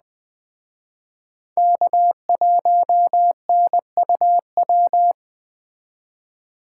Morse code, heard as a keyed tone, about 20 wpm, 700 Hz. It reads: K1NUW